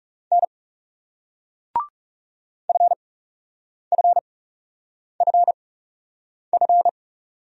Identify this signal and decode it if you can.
Morse code, heard as a keyed tone, about 44 wpm, spaced as for 8 wpm, 700 Hz. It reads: N FFFF